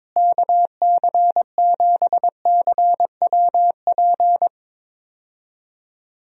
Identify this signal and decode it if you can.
Morse code, heard as a keyed tone, about 22 wpm, 700 Hz. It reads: KC7CWP
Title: USA Call Signs - 6 Characters - Repeated 3x - 22wpm